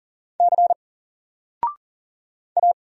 Morse code, heard as a keyed tone, about 40 wpm, 700 Hz. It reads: C A
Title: Single Letter-Number 40wpm